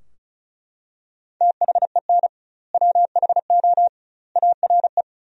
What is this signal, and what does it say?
Morse code, heard as a keyed tone, about 35 wpm, 700 Hz. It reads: THEN WHO ARE
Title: 2x Word-Spacing - Sentences from Top 100 Words - 35wpm